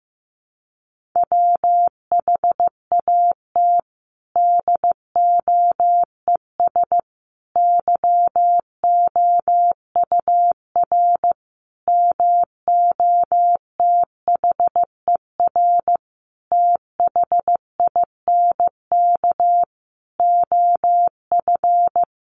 Morse code, heard as a keyed tone, about 15 wpm, 700 Hz. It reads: WHAT DOES YOUR MOTHER THINK OF
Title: Sentences from Top 1400 Words 15wpm